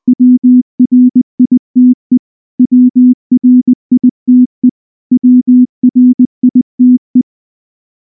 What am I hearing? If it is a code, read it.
Morse code, heard as a keyed tone, about 19 wpm, 250 Hz. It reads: WRITE WRITE WRITE
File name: morse / morse-training-20WPM-6